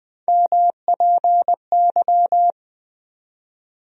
Morse code, heard as a keyed tone, about 20 wpm, 700 Hz. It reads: MPY
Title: USA Call Sign Postfixes - 3 Letters 20wpm